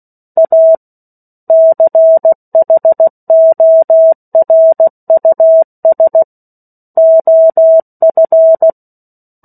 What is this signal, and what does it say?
Morse code, heard as a keyed tone, about 16 wpm, 650 Hz. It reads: A CHORUS OF
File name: 03-Chapter-03 16 WPM